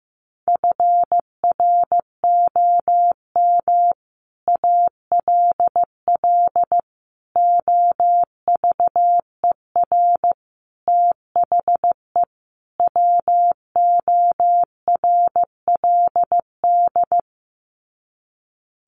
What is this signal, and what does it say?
Morse code, heard as a keyed tone, about 15 wpm, 700 Hz. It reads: FROM ALL OVER THE WORLD
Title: Sets of 5 Words 15wpm